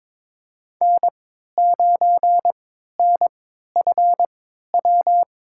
Morse code, heard as a keyed tone, about 22 wpm, spaced as for 13 wpm, 700 Hz. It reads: N9NFW
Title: USA Call Signs - 5 Characters - 3x Character Spacing - 22wpm